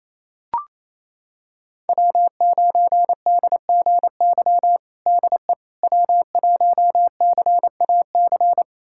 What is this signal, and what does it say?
Morse code, heard as a keyed tone, about 28 wpm, 700 Hz. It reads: W9DGY DE W1CAC
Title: USA Callsign DE Callsign - 5 Characters - 28wpm